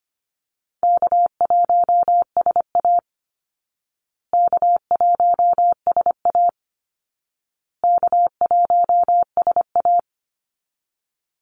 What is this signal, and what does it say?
Morse code, heard as a keyed tone, about 25 wpm, 700 Hz. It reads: K1SA K1SA K1SA
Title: Easy Call Signs Repeated 3x - 25wpm